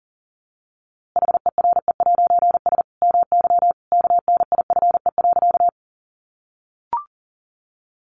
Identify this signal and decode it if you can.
Morse code, heard as a keyed tone, about 40 wpm, 700 Hz. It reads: HERE'S MY KNIFE.
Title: A Sound Of Thunder 40wpm